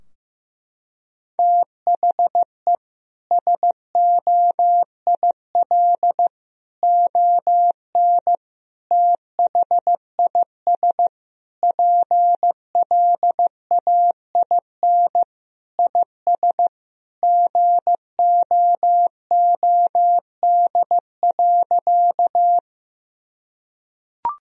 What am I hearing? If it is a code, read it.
Morse code, heard as a keyed tone, about 15 wpm, 700 Hz. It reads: THE SOIL ON THIS PLAIN IS GOOD.
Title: Sentences from Top 1000 Words - Review Part 4 - 15wpm